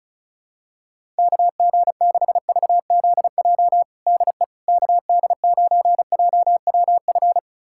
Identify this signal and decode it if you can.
Morse code, heard as a keyed tone, about 36 wpm, 700 Hz. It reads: KG6VZJ DE KD9JWF
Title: USA Callsign DE Callsign - 6 Characters - 35wpm